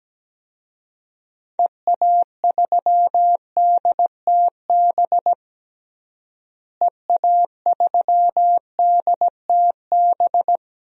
Morse code, heard as a keyed tone, about 17 wpm, 700 Hz. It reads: EA3DTB EA3DTB
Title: DXCC-100 Call Signs - 6 Characters - Repeated 3x - 17wpm